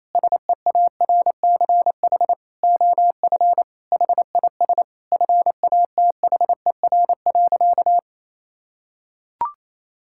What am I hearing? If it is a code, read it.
Morse code, heard as a keyed tone, about 28 wpm, 700 Hz. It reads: SEARCH OF HIS FATHER.
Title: Sentences from Top 1000 Words - Review Part 5 - 28wpm